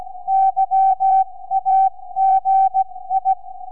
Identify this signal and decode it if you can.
Morse code, heard as a keyed tone, about 16 wpm, spaced as for 14 wpm, 770 Hz. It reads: YAGI
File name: sv5four 4m